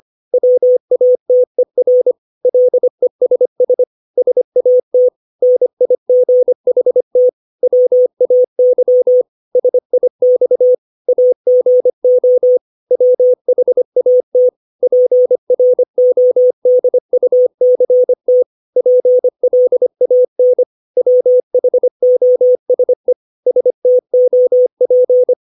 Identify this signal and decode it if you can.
Morse code, heard as a keyed tone, about 25 wpm, 500 Hz. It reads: WATER LESS SAT NIGHT WAY SIX AGO WHAT PRODUCT PLAN WHOSE STOP
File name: 632 words / 632 Words 25WPM